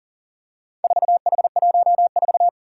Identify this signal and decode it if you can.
Morse code, heard as a keyed tone, about 40 wpm, 700 Hz. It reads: 4H14
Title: Four Challenging Letter-Number Combinations 40wpm